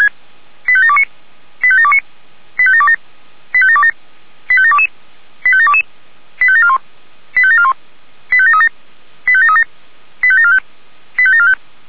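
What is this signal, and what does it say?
Five-tone selective call (ZVEI), heard as quick runs of five tones, tones of about 70 ms. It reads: 85729 85729 85727 85727 85722 85722 85741 85741 85737 85737 85745 85745
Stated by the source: Five-tone ZVEI selective call 85726 received by SDR.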